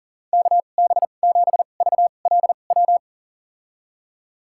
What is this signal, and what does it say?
Morse code, heard as a keyed tone, about 40 wpm, spaced as for 29 wpm, 690 Hz. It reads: KB7VLW
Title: USA Call Signs - 6 Characters - 2x Character Spacing - 40wpm